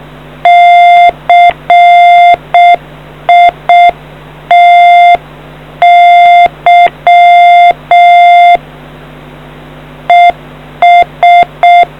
Morse code, heard as a keyed tone, about 6 wpm, 700 Hz. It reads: CITY ES